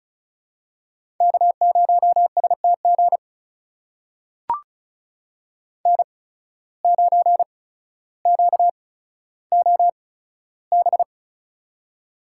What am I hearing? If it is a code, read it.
Morse code, heard as a keyed tone, about 36 wpm, spaced as for 15 wpm, 700 Hz. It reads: K0STG N9QOB
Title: USA Call Signs - 5 Characters - 8x Character Spacing - 35wpm